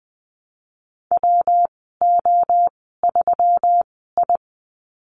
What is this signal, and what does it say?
Morse code, heard as a keyed tone, about 20 wpm, spaced as for 15 wpm, 700 Hz. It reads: WO3I
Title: 128 Lesson 21g - Number 6 - Call Signs 20wpm